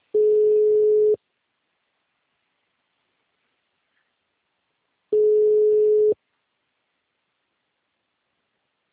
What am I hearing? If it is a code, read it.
Ringback tone (Europe, 425 Hz).